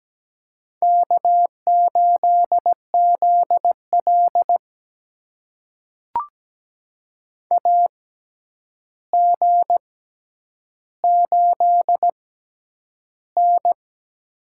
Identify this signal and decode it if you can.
Morse code, heard as a keyed tone, about 17 wpm, spaced as for 9 wpm, 700 Hz. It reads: K8ZL AG8N
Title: USA Call Signs - 4 Characters - 6x Character Spacing - 17wpm